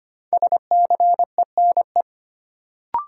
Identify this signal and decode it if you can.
Morse code, heard as a keyed tone, about 25 wpm, 700 Hz. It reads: SCENE
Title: Sentences from Top 1100 Words - New Word Drill - 25wpm